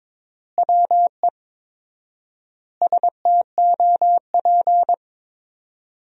Morse code, heard as a keyed tone, about 22 wpm, 700 Hz. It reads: WE STOP
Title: 4x Word-Spacing - Sentences from Top 300 Words - 22wpm